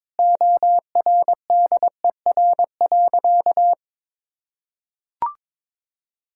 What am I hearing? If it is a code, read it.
Morse code, heard as a keyed tone, about 22 wpm, 700 Hz. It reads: ORDER.
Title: Sentences from Top 1600 Words 22wpm